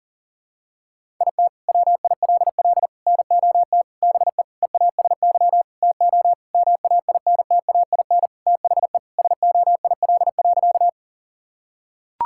Morse code, heard as a keyed tone, about 40 wpm, 700 Hz. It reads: IT WILL NOT BE EASY TO MAINTAIN THE SOIL.
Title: Sentences from Top 1000 Words - Review Part 5 - 40wpm